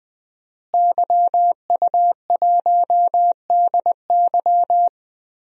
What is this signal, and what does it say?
Morse code, heard as a keyed tone, about 20 wpm, 700 Hz. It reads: YU1DY
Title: DXCC-100 Call Signs - 5 Characters - Repeated 3x - 20wpm